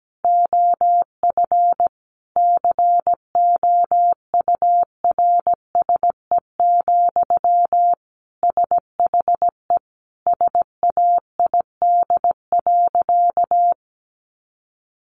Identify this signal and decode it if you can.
Morse code, heard as a keyed tone, about 17 wpm, 700 Hz. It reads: OF COURSE, SHE SAID.
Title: Sentences from Top 1400 Words 17wpm